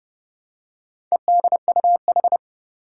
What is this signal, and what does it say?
Morse code, heard as a keyed tone, about 32 wpm, 700 Hz. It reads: EDUH
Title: Four Challenging Letter-Number Combinations 30wpm